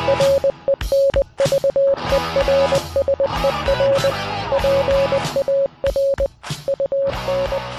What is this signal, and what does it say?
Morse code, heard as a keyed tone, about 20 wpm, 550 Hz. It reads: RENVERSER PAR UN